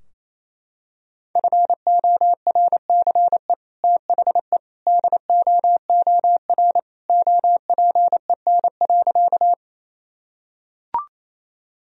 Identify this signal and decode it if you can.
Morse code, heard as a keyed tone, about 28 wpm, 700 Hz. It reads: FORCE THE DOOR OPEN.
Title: Sentences from Top 1000 Words - Review Part 3 - 28wpm